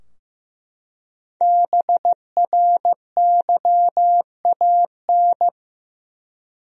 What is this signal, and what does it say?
Morse code, heard as a keyed tone, about 15 wpm, 700 Hz. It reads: BRYAN
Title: USA Common Names - 15wpm